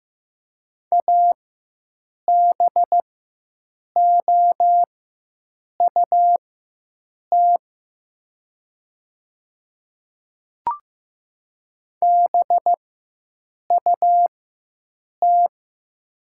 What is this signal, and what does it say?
Morse code, heard as a keyed tone, about 15 wpm, spaced as for 7 wpm, 700 Hz. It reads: ABOUT BUT
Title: 4x Character Spacing - Top 200 Words - 15wpm